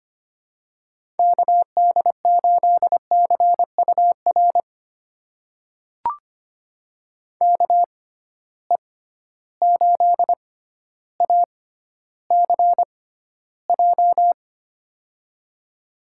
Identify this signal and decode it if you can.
Morse code, heard as a keyed tone, about 25 wpm, spaced as for 13 wpm, 700 Hz. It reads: KD8CUR KE8ACJ
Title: USA Call Signs - 6 Characters - 6x Character Spacing - 25wpm